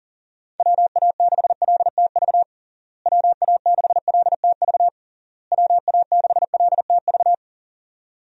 Morse code, heard as a keyed tone, about 40 wpm, 700 Hz. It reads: WA6LTV WA6LTV WA6LTV
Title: USA Call Signs - 6 Characters - Repeated 3x - 40wpm